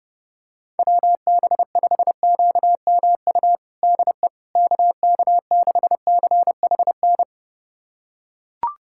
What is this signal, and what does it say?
Morse code, heard as a keyed tone, about 32 wpm, 700 Hz. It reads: WB5QMU DE KK6CHN
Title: USA Callsign DE Callsign 30wpm